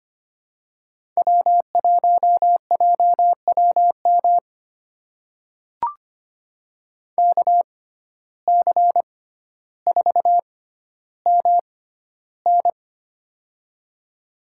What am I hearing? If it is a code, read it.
Morse code, heard as a keyed tone, about 25 wpm, spaced as for 13 wpm, 700 Hz. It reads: W1JWM KC4MN
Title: USA Call Signs - 5 Characters - 6x Character Spacing - 25wpm